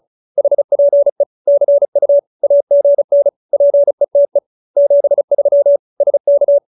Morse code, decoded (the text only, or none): HPE CU AGN PETE 73 SK